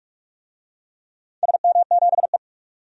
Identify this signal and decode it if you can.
Morse code, heard as a keyed tone, about 44 wpm, 700 Hz. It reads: SM7E
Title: DXCC-100 Call Signs - 4 Characters - 45wpm